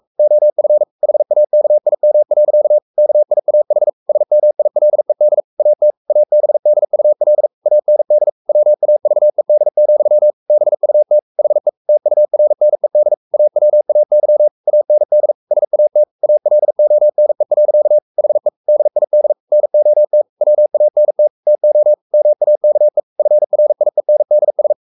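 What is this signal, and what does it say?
Morse code, heard as a keyed tone, about 44 wpm, 600 Hz. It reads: OF HAKIM. KIAH SMILED AT ABDUL AND WAVED, BUT HE TURNED AWAY AND SAT ALONE. HE DID NOT WANT TO MAKE FRIENDS
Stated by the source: Morse code at 43 wpm sending Ken Follett's novel Never in English, from CW med Gnister.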